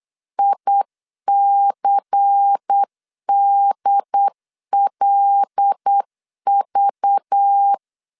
Morse code, heard as a keyed tone, about 8 wpm, 800 Hz. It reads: ICDLV
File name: CW / DET / 1 9 1500C S A 8wpm 5637